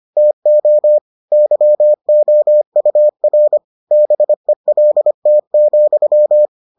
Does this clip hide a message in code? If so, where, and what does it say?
Yes: Morse code from 0.2 to 6.5 s, heard as a keyed tone, about 25 wpm, 600 Hz. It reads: TO YOUR BELT,